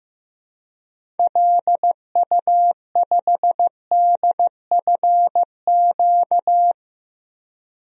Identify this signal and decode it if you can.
Morse code, heard as a keyed tone, about 15 wpm, 700 Hz. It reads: LU5DFQ
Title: DXCC-100 Call Signs - 6 Characters - 15wpm